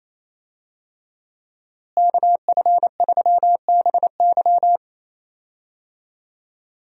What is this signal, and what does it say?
Morse code, heard as a keyed tone, about 28 wpm, 700 Hz. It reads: KF3BY
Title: USA Call Signs - 5 Characters - Repeated 3x - 28wpm